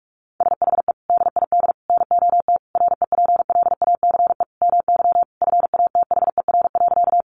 Morse code, heard as a keyed tone, about 44 wpm, 700 Hz. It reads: SHE DID NOT REPLACE MY FATHER.
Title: Sentences from Top 1400 Words 45wpm